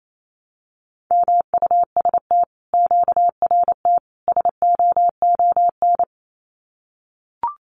Morse code, heard as a keyed tone, about 28 wpm, 700 Hz. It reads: MUST QRT SOON
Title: Common QSO Phrases Encore 28wpm